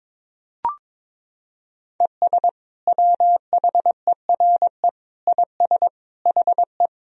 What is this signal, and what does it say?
Morse code, heard as a keyed tone, about 21 wpm, 700 Hz. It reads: ES WHERE IS HE
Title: Sentences from Top 200 Words and Abbreviations 22wpm